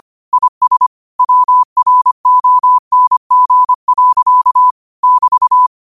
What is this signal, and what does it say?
Morse code, heard as a keyed tone, about 25 wpm, 1000 Hz. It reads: IS WRONG. <BT>